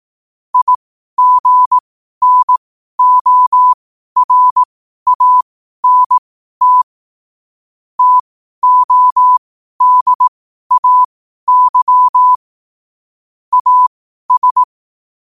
Morse code, heard as a keyed tone, about 18 wpm, spaced as for 12 wpm, 1000 Hz. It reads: IGNORANT TODAY AS
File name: qotdfiles / 12wpm